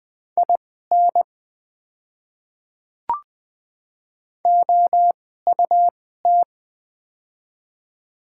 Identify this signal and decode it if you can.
Morse code, heard as a keyed tone, about 20 wpm, spaced as for 14 wpm, 700 Hz. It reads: IN OUT
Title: 2x Character Spacing - Top 100 Words - 20wpm